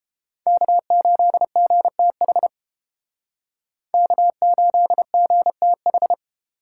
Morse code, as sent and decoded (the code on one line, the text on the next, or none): -.- ---.. --. - .... / -.- ---.. --. - ....
K8GTH K8GTH